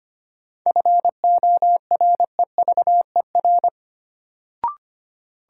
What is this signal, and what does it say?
Morse code, heard as a keyed tone, about 25 wpm, 700 Hz. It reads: FOREVER
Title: Sentences from Top 1600 Words - New Word Drill - 25wpm